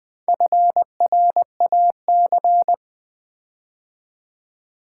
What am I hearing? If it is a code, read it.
Morse code, heard as a keyed tone, about 20 wpm, 700 Hz. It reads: FRAC
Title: Build A Word - 4 Letters At A Time - 8-Letter Words - 20wpm